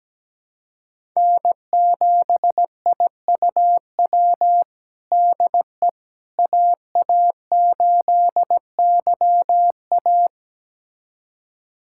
Morse code, heard as a keyed tone, about 17 wpm, 700 Hz. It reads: N7IUW DE AA8YA
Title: USA Callsign DE Callsign - 5 Characters - 17wpm